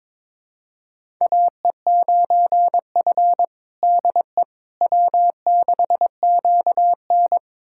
Morse code, heard as a keyed tone, about 22 wpm, 700 Hz. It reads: AE9F DE W6QN
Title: USA Callsign DE Callsign - 4 Characters - 22wpm